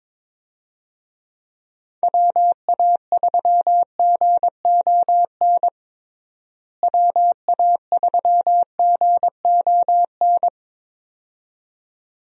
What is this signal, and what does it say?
Morse code, heard as a keyed tone, about 22 wpm, 700 Hz. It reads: WA3GON WA3GON
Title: USA Call Signs - 6 Characters - Repeated 2x - 22wpm